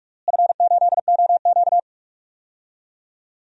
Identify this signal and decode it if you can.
Morse code, heard as a keyed tone, about 42 wpm, 700 Hz. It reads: F8OQ
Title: DXCC-100 Call Signs - 4 Characters - Repeated 3x - 45wpm